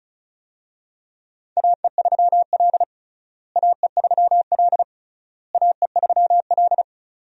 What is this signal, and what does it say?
Morse code, heard as a keyed tone, about 34 wpm, 700 Hz. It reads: AE3L AE3L AE3L
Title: USA Call Signs - 4 Characters - Repeated 3x - 35wpm